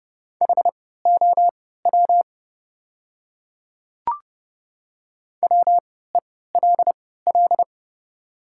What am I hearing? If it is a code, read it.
Morse code, heard as a keyed tone, about 30 wpm, spaced as for 17 wpm, 700 Hz. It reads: HOW WELL
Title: 3x Character Spacing - Top 200 Words - 30wpm